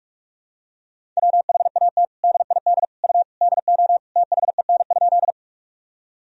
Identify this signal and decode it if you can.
Morse code, heard as a keyed tone, about 46 wpm, 700 Hz. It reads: WHAT DID U DO THEN?
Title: Sentences from Top 100 Words and Abbreviations 45wpm